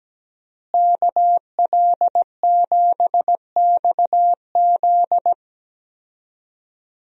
Morse code, heard as a keyed tone, about 17 wpm, 700 Hz. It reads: KL7XZ
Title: DXCC-100 Call Signs - 5 Characters - Repeated 3x - 17wpm